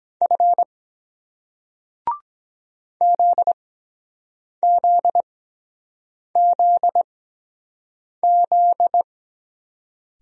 Morse code, heard as a keyed tone, about 23 wpm, spaced as for 7 wpm, 700 Hz. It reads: F ZZZZ